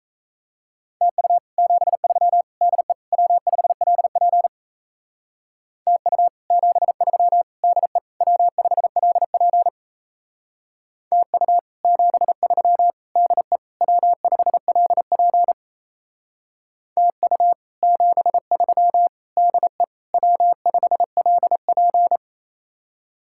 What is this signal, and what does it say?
Morse code, heard as a keyed tone, about 37 wpm, 700 Hz. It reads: TU 73 DE W5LP TU 73 DE W5LP TU 73 DE W5LP TU 73 DE W5LP